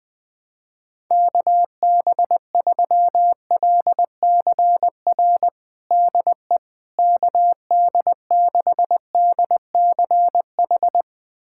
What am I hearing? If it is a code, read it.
Morse code, heard as a keyed tone, about 21 wpm, 700 Hz. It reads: KB3LCR DE KD6DCH
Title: USA Callsign DE Callsign - 6 Characters - 20wpm